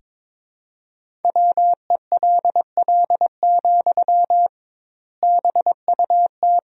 Morse code, heard as a keyed tone, about 21 wpm, 700 Hz. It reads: WELL, BUT
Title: 2x Word-Spacing - Sentences from Top 500 Words - 22wpm